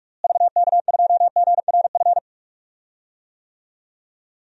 Morse code, heard as a keyed tone, about 46 wpm, 700 Hz. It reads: VK2GRF
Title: DXCC-100 Call Signs - 6 Characters - Repeated 3x - 45wpm